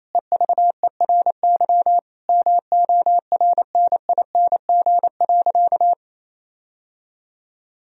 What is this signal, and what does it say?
Morse code, heard as a keyed tone, about 28 wpm, 700 Hz. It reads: EVERY MORNING.
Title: Sentences from Top 1000 Words - Review Part 3 - 28wpm